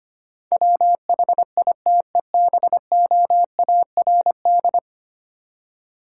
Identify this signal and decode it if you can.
Morse code, heard as a keyed tone, about 25 wpm, 700 Hz. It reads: WHITEBOARD
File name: practice / 25wpm / Easy Words with Hints - 25wpm